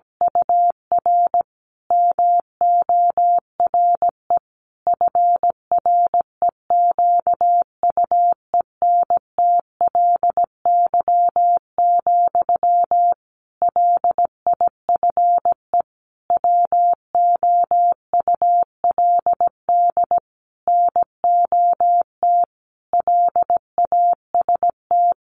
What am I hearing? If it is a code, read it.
Morse code, heard as a keyed tone, about 17 wpm, 700 Hz. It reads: UR MORE FREQUENTLY, LIFE WOULD NOT LAST
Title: Sentences from Top 1300 Words 17wpm